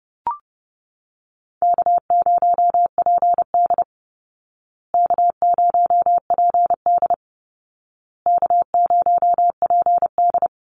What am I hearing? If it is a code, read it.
Morse code, heard as a keyed tone, about 30 wpm, 700 Hz. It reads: K0PD K0PD K0PD
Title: Easy Call Signs Repeated 3x - 30wpm